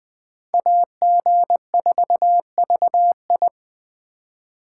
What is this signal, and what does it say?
Morse code, heard as a keyed tone, about 20 wpm, 700 Hz. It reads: AG4VI
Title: USA Call Signs - 5 Characters - 2x Character Spacing - 20wpm